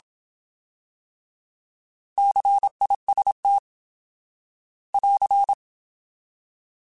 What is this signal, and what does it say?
Morse code, heard as a keyed tone, about 27 wpm, 800 Hz. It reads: CIST <AR>
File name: Alphabetical / Alphabetical mp3s / Cs @25